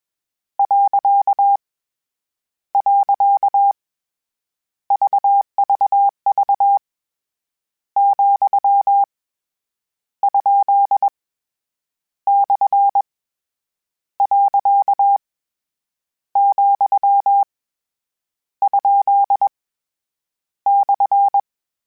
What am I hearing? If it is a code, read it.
Morse code, heard as a keyed tone, about 21 wpm, 800 Hz. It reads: . VVV , ? / . , ? /
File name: Misc / Misc-mp3s / Punct @20